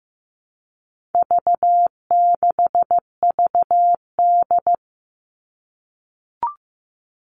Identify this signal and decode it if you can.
Morse code, heard as a keyed tone, about 15 wpm, 700 Hz. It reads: V6VD
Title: All Cut Numbers - 4-Digit 15wpm